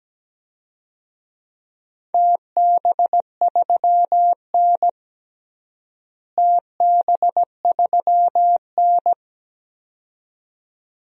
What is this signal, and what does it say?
Morse code, heard as a keyed tone, about 17 wpm, 700 Hz. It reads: TB3N TB3N
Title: DXCC-100 Call Signs - 4 Characters - Repeated 2x - 17wpm